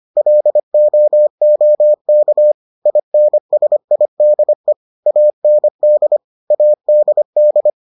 Morse code, heard as a keyed tone, about 25 wpm, 600 Hz. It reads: LOOK INSIDE AND ADD